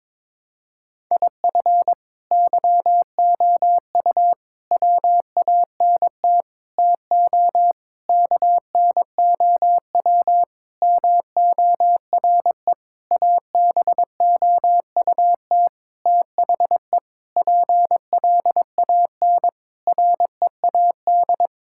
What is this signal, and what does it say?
Morse code, heard as a keyed tone, about 22 wpm, 700 Hz. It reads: IF YOU WANT TO KNOW MORE ABOUT THE PLAN READ
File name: practice / 22wpm / Sentences with 8-Word Phrases - 22wpm